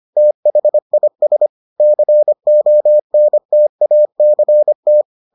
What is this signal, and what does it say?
Morse code, heard as a keyed tone, about 25 wpm, 600 Hz. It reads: THIS CONTACT